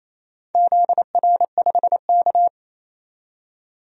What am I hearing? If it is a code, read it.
Morse code, heard as a keyed tone, about 27 wpm, 700 Hz. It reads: ZR5K
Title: DXCC-100 Call Signs - 4 Characters - 28wpm